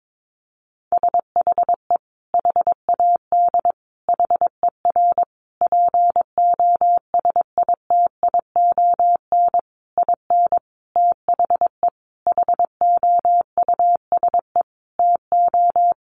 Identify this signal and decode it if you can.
Morse code, heard as a keyed tone, about 22 wpm, 700 Hz. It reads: SHE HAD HER POSITION IN THE HOUSE TO
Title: Sentences from Top 900 Words 22wpm